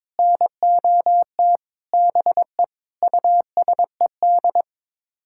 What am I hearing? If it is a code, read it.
Morse code, heard as a keyed tone, about 22 wpm, 700 Hz. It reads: NOT BE USED
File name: Sets of 9 Words - 22wpm